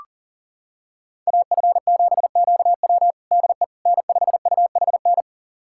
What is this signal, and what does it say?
Morse code, heard as a keyed tone, about 40 wpm, 700 Hz. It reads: AF7QW DE N5UHN
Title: USA Callsign DE Callsign - 5 Characters - 40wpm